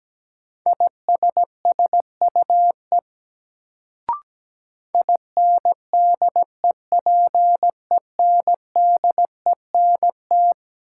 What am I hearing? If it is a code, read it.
Morse code, heard as a keyed tone, about 17 wpm, 700 Hz. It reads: ISSUE INDEPENDENT
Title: Sentences from Top 900 Words - New Word Drill - 17wpm